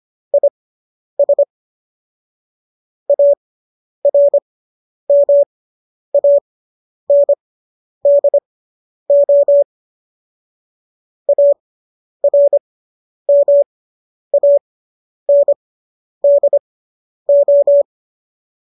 Morse code, heard as a keyed tone, about 25 wpm, spaced as for 10 wpm, 570 Hz. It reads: IS ARMANDO ARMANDO